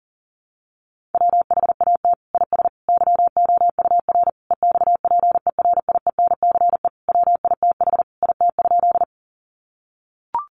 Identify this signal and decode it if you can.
Morse code, heard as a keyed tone, about 40 wpm, 700 Hz. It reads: WHAT IS YOUR EXPERIENCE WITH IT?
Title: Sentences from Top 600 Words 40wpm